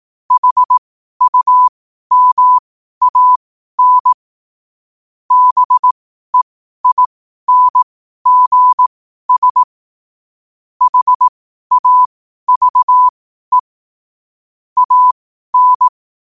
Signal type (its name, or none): Morse code